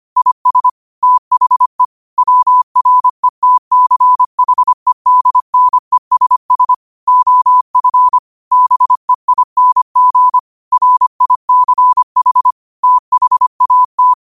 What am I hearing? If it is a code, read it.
Morse code, heard as a keyed tone, about 25 wpm, 1000 Hz. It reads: IS THE WRETCHEDNESS OF BEING RICH THAT
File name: qotdfiles / 25wpm